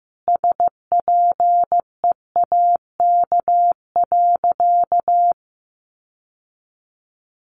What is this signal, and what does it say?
Morse code, heard as a keyed tone, about 14 wpm, 700 Hz. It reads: SPEAK.
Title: Sentences from Top 1500 Words 15wpm